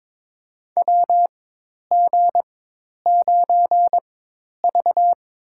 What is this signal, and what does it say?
Morse code, heard as a keyed tone, about 22 wpm, spaced as for 10 wpm, 700 Hz. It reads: WG9V
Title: USA Call Signs - 4 Characters - 4x Character Spacing - 22wpm